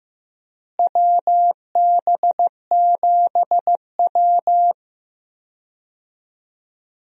Morse code, heard as a keyed tone, about 15 wpm, 700 Hz. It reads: WB7W
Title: USA Call Signs - 4 Characters - Repeated 3x - 15wpm